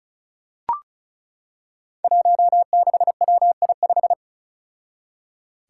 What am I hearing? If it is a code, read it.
Morse code, heard as a keyed tone, about 36 wpm, 700 Hz. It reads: 16WI5
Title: Five Challenging Letter-Number Combinations 35wpm